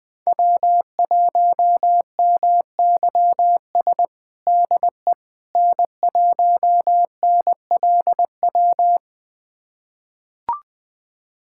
Morse code, heard as a keyed tone, about 20 wpm, 700 Hz. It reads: W1MYS DE N1NLW
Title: USA Callsign DE Callsign - 5 Characters - 20wpm